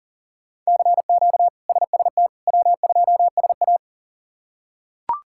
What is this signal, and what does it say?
Morse code, heard as a keyed tone, about 40 wpm, 700 Hz. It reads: CQ SST W2SA
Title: K1USN SST QSO Practice - 40wpm